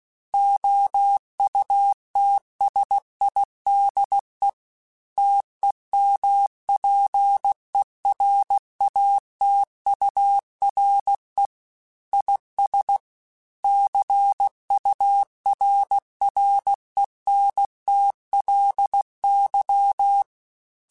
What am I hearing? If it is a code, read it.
Morse code, heard as a keyed tone, about 16 wpm, 800 Hz. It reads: OUTSIDE TEMPERATURE IS CURRENTLY